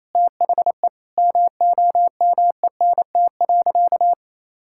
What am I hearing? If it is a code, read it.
Morse code, heard as a keyed tone, about 28 wpm, 700 Hz. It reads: THE MOMENT.
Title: Sentences from Top 1000 Words - Review Part 5 - 28wpm